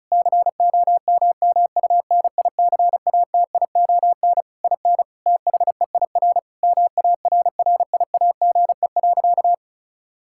Morse code, heard as a keyed tone, about 34 wpm, 700 Hz. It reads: COMMUNICATION IN THEIR MARRIAGE.
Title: Sentences from Top 1000 Words - Review Part 6 - 35wpm